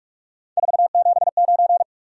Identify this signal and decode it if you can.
Morse code, heard as a keyed tone, about 47 wpm, 700 Hz. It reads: 479
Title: Top 180 QSO Elements 45wpm